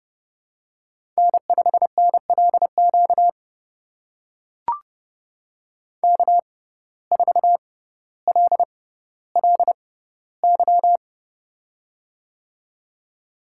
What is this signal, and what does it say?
Morse code, heard as a keyed tone, about 31 wpm, spaced as for 16 wpm, 700 Hz. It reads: N5NLQ K4LLY
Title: USA Call Signs - 5 Characters - 6x Character Spacing - 30wpm